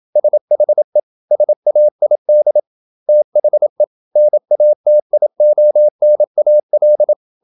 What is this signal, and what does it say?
Morse code, heard as a keyed tone, about 27 wpm, 600 Hz. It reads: SHE SAID THE NATIONAL
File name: Never 0027wpm